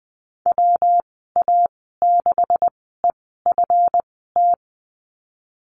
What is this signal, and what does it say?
Morse code, heard as a keyed tone, about 20 wpm, spaced as for 15 wpm, 700 Hz. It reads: WA6EFT
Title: 161 Lesson 26g - Letter g - Call Signs 20wpm